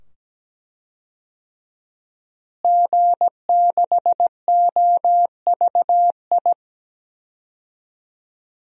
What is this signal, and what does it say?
Morse code, heard as a keyed tone, about 17 wpm, 700 Hz. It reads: G6OVI
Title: DXCC-100 Call Signs - 5 Characters - Repeated 2x - 17wpm